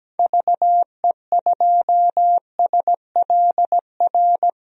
Morse code, heard as a keyed tone, about 17 wpm, 700 Hz. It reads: VE2SLR
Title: DXCC-100 Call Signs - 6 Characters - 17wpm